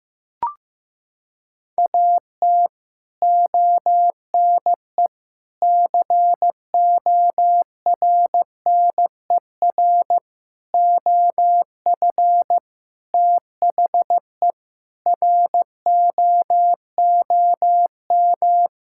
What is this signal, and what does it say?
Morse code, heard as a keyed tone, about 15 wpm, 700 Hz. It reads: AT ONE CORNER OF THE ROOM
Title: Sentences from Top 1600 Words 15wpm